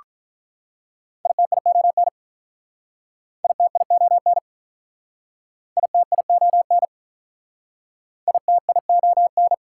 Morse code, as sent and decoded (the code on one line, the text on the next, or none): .. - .. --- -. / .. - .. --- -. / .. - .. --- -. / .. - .. --- -.
ITION ITION ITION ITION